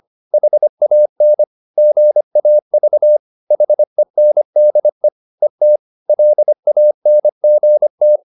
Morse code, heard as a keyed tone, about 25 wpm, 600 Hz. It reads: HAN GAV HENDE ET LANGT
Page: Ferie 25 wpm